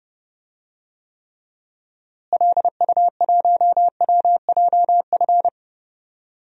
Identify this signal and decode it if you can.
Morse code, heard as a keyed tone, about 30 wpm, 700 Hz. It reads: LU1WJF
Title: DXCC-100 Call Signs - 6 Characters - Repeated 2x - 30wpm